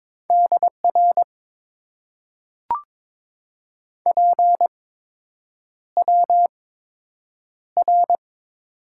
Morse code, heard as a keyed tone, about 23 wpm, spaced as for 8 wpm, 690 Hz. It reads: DR PWR
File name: Top 90 QSO Elements - 8x Character Spacing - 22wpm